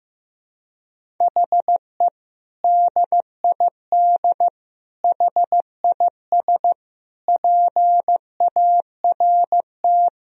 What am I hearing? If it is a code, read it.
Morse code, heard as a keyed tone, about 15 wpm, 700 Hz. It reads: HE DID HIS PART
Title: Sentences from Top 1000 Words - Review Part 1 - 15wpm